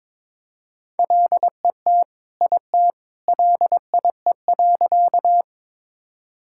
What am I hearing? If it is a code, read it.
Morse code, heard as a keyed tone, about 21 wpm, 700 Hz. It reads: LET IT LIE.
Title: Sentences from Top 1600 Words 22wpm